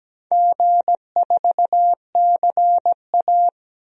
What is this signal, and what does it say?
Morse code, heard as a keyed tone, about 17 wpm, 700 Hz. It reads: G4CA
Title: DXCC-100 Call Signs - 4 Characters - 17wpm